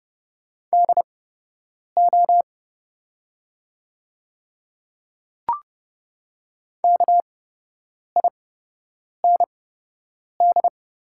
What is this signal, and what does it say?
Morse code, heard as a keyed tone, about 30 wpm, spaced as for 8 wpm, 700 Hz. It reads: DO KIND
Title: 8x Character Spacing - Top 200 Words - 30wpm